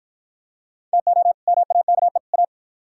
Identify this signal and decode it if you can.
Morse code, heard as a keyed tone, about 48 wpm, 700 Hz. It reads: TO MAKE A